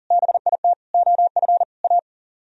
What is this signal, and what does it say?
Morse code, heard as a keyed tone, about 52 wpm, 700 Hz. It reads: BIT OF A